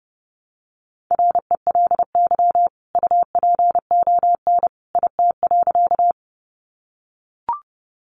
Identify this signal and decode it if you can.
Morse code, heard as a keyed tone, about 30 wpm, 700 Hz. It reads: RELY UPON IT.
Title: Sentences from Top 1200 Words 30wpm